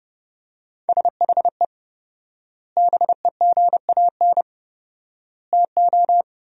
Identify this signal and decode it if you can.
Morse code, heard as a keyed tone, about 30 wpm, 700 Hz. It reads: SHE BEGAN TO
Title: 4x Word-Spacing - Sentences from Top 500 Words - 30wpm